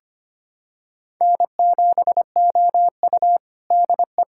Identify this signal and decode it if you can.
Morse code, heard as a keyed tone, about 25 wpm, 700 Hz. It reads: N7OU DE
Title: USA Callsign DE Callsign - 4 Characters - 25wpm